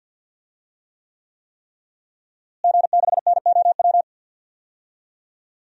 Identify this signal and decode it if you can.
Morse code, heard as a keyed tone, about 52 wpm, 700 Hz. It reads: G6NOW
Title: DXCC-100 Call Signs - 5 Characters - Repeated 3x - 50wpm